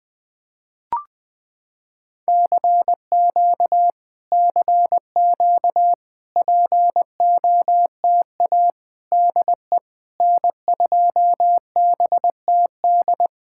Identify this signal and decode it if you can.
Morse code, heard as a keyed tone, about 20 wpm, 700 Hz. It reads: CQ CQ POTA DE N2BTD
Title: POTA QSO Practice - 20wpm